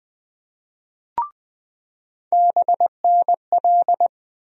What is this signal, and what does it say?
Morse code, heard as a keyed tone, about 20 wpm, 700 Hz. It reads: BNL
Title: USA Call Sign Postfixes - 3 Letters 20wpm